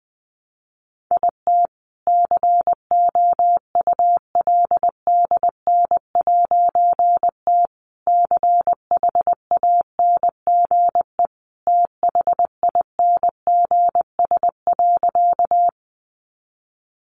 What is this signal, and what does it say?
Morse code, heard as a keyed tone, about 20 wpm, 700 Hz. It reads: IT COULDN'T CHANGE THINGS.
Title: A Sound Of Thunder 20wpm